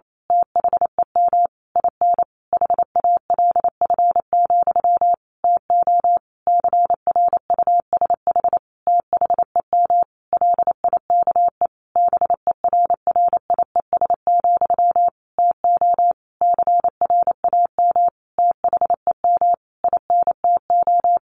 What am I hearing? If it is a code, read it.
Morse code, heard as a keyed tone, about 28 wpm, 700 Hz. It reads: THEM IN HALF, TO CRUSH THEM LIKE BERRIES, TO CRAM THEM INTO
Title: A Sound Of Thunder 28wpm